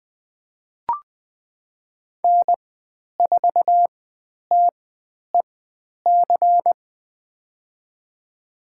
Morse code, heard as a keyed tone, about 20 wpm, spaced as for 10 wpm, 700 Hz. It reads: N4TEC
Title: 109 Lesson 18g - Letter m - Call Signs 20wpm